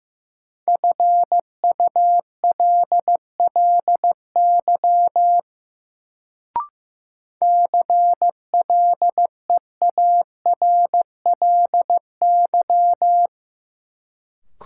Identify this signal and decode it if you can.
Morse code, heard as a keyed tone, about 15 wpm, 700 Hz. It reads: FULLY CLEARLY
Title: Sentences from Top 1300 Words - New Word Drill - 15wpm